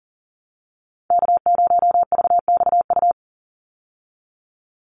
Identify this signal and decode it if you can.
Morse code, heard as a keyed tone, about 40 wpm, 700 Hz. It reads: K0VXU
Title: Call Signs Repeated 3x - 40wpm